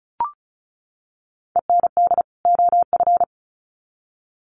Morse code, heard as a keyed tone, about 34 wpm, 700 Hz. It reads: END OF
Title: Sets of 2 Words 35wpm